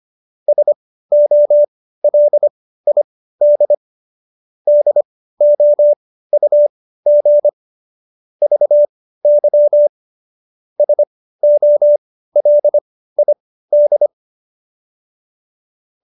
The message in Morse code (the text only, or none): SOLID DOUG VY SOLID